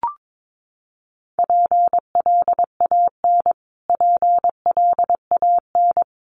Morse code, heard as a keyed tone, about 22 wpm, 700 Hz. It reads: PLAN PLAN
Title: Top 500 Words Repeated 22wpm